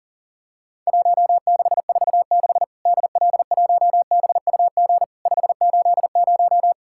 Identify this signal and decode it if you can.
Morse code, heard as a keyed tone, about 40 wpm, 700 Hz. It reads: 1646 DL1BUG 580
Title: Worked All Europe DX Contest - QTC Exchange - 40wpm